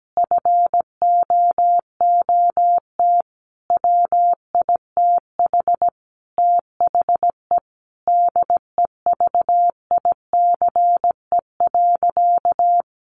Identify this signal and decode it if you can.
Morse code, heard as a keyed tone, about 17 wpm, 700 Hz. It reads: FOOT WITH THE DEVICE.
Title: Sentences from Top 1500 Words 17wpm